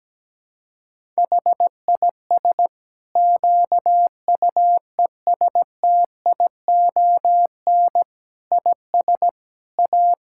Morse code, heard as a keyed tone, about 17 wpm, 700 Hz. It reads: HIS QUESTION IS A
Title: Sentences from Top 1000 Words - Review Part 5 - 17wpm